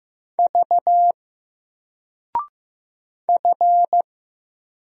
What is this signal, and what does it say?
Morse code, heard as a keyed tone, about 15 wpm, 700 Hz. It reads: V F
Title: Single Character (Letter:Number:Punctuation:Prosign) 15wpm